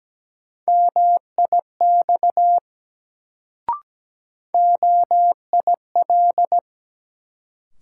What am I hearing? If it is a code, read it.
Morse code, heard as a keyed tone, about 17 wpm, 700 Hz. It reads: MIX OIL
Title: Sentences from Top 1000 Words - Review Part 4 - New Word Drill - 17wpm